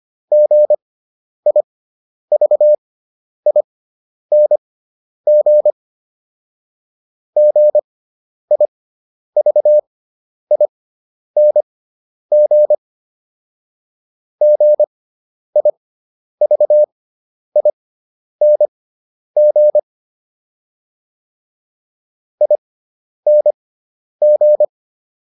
Morse code, heard as a keyed tone, about 25 wpm, spaced as for 10 wpm, 600 Hz. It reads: GIVING GIVING GIVING ING